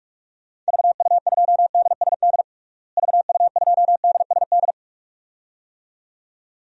Morse code, heard as a keyed tone, about 47 wpm, 700 Hz. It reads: VU2DSD VU2DSD
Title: DXCC-100 Call Signs - 6 Characters - Repeated 2x - 45wpm